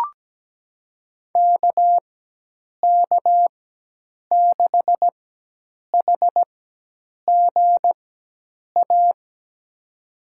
Morse code, heard as a keyed tone, about 17 wpm, spaced as for 8 wpm, 700 Hz. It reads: KK6HGA
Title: USA Call Signs - 6 Characters - 4x Character Spacing - 17wpm